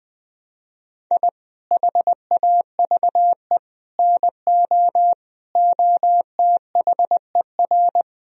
Morse code, heard as a keyed tone, about 20 wpm, 700 Hz. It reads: I HAVE NO OTHER
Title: Sentences from Top 1000 Words - Review Part 3 - 20wpm